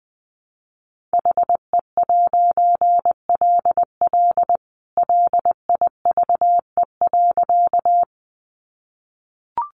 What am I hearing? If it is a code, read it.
Morse code, heard as a keyed tone, about 20 wpm, 700 Hz. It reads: HE'LL LIVE.
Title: A Sound Of Thunder 20wpm